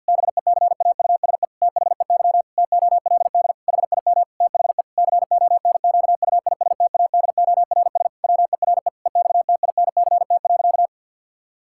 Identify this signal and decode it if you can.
Morse code, heard as a keyed tone, about 50 wpm, 700 Hz. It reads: BECAUSE THEY TOLD HIM THE CONQUISTADORS WERE EXTINCT.
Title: Pangram Challenge v2 - 50wpm